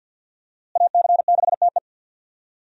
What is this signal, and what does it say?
Morse code, heard as a keyed tone, about 53 wpm, 700 Hz. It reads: AC6TE
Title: USA Call Signs - 5 Characters - Repeated 2x - 50wpm